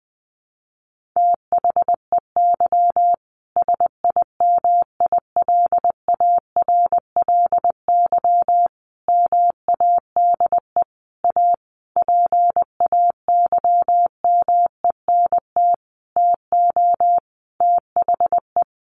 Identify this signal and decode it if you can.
Morse code, heard as a keyed tone, about 20 wpm, 700 Hz. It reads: THEY SIMILARLY MADE A PAYMENT TO THE
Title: Sentences from Top 1400 Words 20wpm